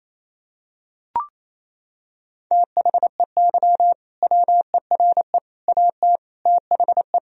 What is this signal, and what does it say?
Morse code, heard as a keyed tone, about 27 wpm, 700 Hz. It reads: THEY WERE AT THE
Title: Sentences from Top 1000 Words - Review Part 3 - 28wpm